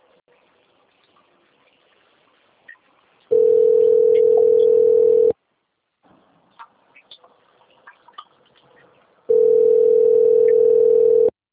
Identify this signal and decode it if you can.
Ringback tone (North America), 440 + 480 Hz.
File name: out- - - - - .5107